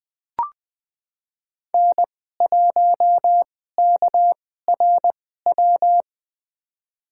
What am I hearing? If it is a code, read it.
Morse code, heard as a keyed tone, about 20 wpm, spaced as for 14 wpm, 700 Hz. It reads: N1KRW
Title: USA Call Signs - 5 Characters - 2x Character Spacing - 20wpm